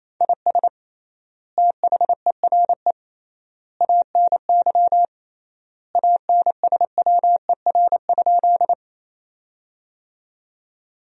Morse code, heard as a keyed tone, about 28 wpm, 700 Hz. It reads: IS THERE ANY ANSWER?
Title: 3x Word-Spacing - Sentences from Top 300 Words - 28wpm